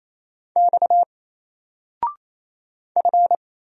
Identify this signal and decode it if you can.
Morse code, heard as a keyed tone, about 27 wpm, 700 Hz. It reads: X F
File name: Single-Letter-Number-28wpm